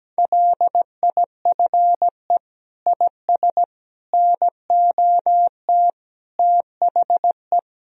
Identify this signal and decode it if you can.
Morse code, heard as a keyed tone, about 17 wpm, 700 Hz. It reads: LIFE IS NOT THE
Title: Sentences from Top 1000 Words - Review Part 3 - 17wpm